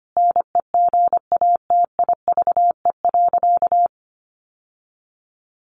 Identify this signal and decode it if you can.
Morse code, heard as a keyed tone, about 25 wpm, 700 Hz. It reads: NEGATIVE.
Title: Sentences from Top 1000 Words 25wpm